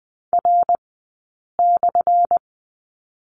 Morse code, heard as a keyed tone, about 20 wpm, spaced as for 8 wpm, 700 Hz. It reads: R/
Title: 211 Lesson 34d - Number 8 - Two Characters 20wpm